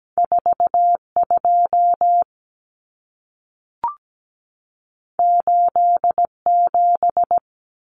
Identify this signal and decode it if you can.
Morse code, heard as a keyed tone, about 17 wpm, 700 Hz. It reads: 42 87
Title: Common Cut Numbers - 2-Digit 17wpm